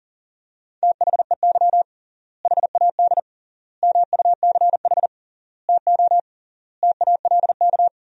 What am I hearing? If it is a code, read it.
Morse code, heard as a keyed tone, about 37 wpm, 700 Hz. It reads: THEY HAD MUCH TO TALK
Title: 3x Word-Spacing - Sentences from Top 400 Words - 40wpm